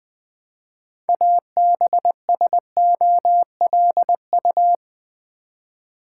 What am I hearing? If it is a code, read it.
Morse code, heard as a keyed tone, about 20 wpm, 690 Hz. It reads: ABSOLU
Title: Build A Word - 3 Letters At A Time - 8-Letter Words - 20wpm